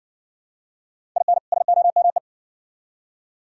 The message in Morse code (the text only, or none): IN SOME